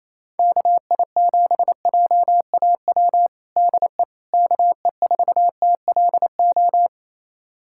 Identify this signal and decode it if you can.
Morse code, heard as a keyed tone, about 28 wpm, 700 Hz. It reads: KI7JAW DE KE4TLO
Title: USA Callsign DE Callsign 28wpm